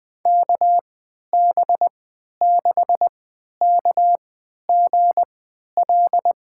Morse code, heard as a keyed tone, about 20 wpm, spaced as for 11 wpm, 700 Hz. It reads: KB6KGL
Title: USA Call Signs - 3x Character Spacing - 20wpm